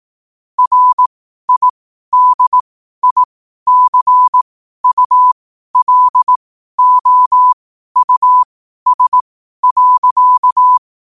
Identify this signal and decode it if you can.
Morse code, heard as a keyed tone, about 18 wpm, spaced as for 12 wpm, 1000 Hz. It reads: RIDICULOUS.
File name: qotdfiles / 12wpm